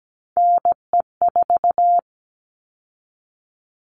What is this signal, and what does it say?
Morse code, heard as a keyed tone, about 17 wpm, 700 Hz. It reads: NE4
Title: All Cut Numbers - 3-Digit 17wpm